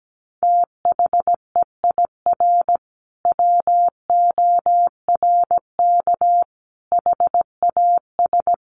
Morse code, heard as a keyed tone, about 17 wpm, 700 Hz. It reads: THEIR WORK HAS
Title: Sentences from Top 1500 Words 17wpm